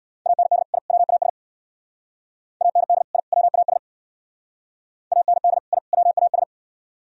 Morse code, heard as a keyed tone, about 62 wpm, 700 Hz. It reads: AND I WAS AND I WAS AND I WAS
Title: Speed-Racing - Sets of 3 Words 50wpm